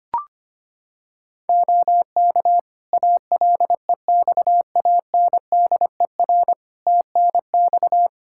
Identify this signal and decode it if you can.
Morse code, heard as a keyed tone, about 25 wpm, 700 Hz. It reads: OK ALEXANDER TNX
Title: K1USN SST QSO Practice 25wpm